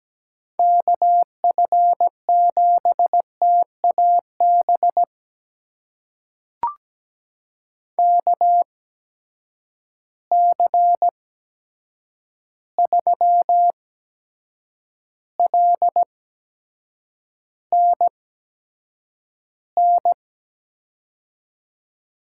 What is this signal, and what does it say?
Morse code, heard as a keyed tone, about 17 wpm, spaced as for 7 wpm, 700 Hz. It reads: KF7TAB KC3LNN
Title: USA Call Signs - 6 Characters - 8x Character Spacing - 17wpm